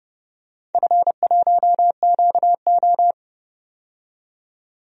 Morse code, heard as a keyed tone, about 30 wpm, 700 Hz. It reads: F1QO
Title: DXCC-100 Call Signs - 4 Characters - 30wpm